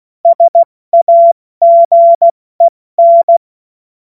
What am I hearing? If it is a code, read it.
Morse code, heard as a keyed tone, about 16 wpm, 680 Hz. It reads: SAGEN
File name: Morsekurs / Witze / Witz 10. Januar 2023 70 F80